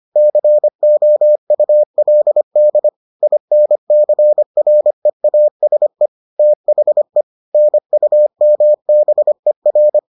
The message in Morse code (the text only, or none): COULD INCREASE THE NUMBER